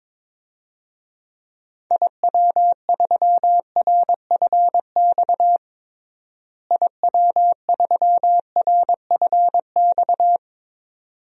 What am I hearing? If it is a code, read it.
Morse code, heard as a keyed tone, about 22 wpm, 700 Hz. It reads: IW3RFX IW3RFX
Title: DXCC-100 Call Signs - 6 Characters - Repeated 2x - 22wpm